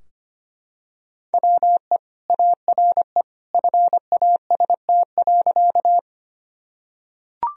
Morse code, heard as a keyed tone, about 25 wpm, 700 Hz. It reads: WE ARE FAST.
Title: Sentences from Top 1000 Words - Review Part 2 - 25wpm